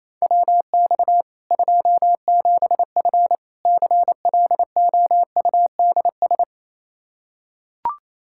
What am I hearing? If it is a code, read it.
Morse code, heard as a keyed tone, about 28 wpm, 700 Hz. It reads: WX 27F CLOUDS
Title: Practice QSO - Exchange 2 - Info - 28wpm